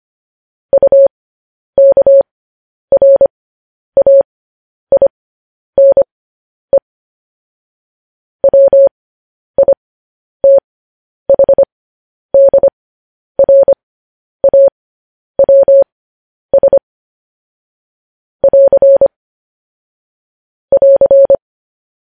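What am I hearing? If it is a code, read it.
Morse code, heard as a keyed tone, about 25 wpm, spaced as for 10 wpm, 550 Hz. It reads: UKRAINE WITHDRAWS <AR> <AR>
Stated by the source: Morse code news headlines sent at 10 wpm for practice.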